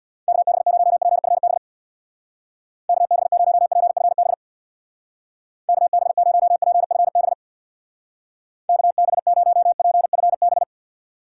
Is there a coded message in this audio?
Yes: Morse code from 0.3 to 10.6 s, heard as a keyed tone, about 64 wpm, 700 Hz. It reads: KB0PFB KB0PFB KB0PFB KB0PFB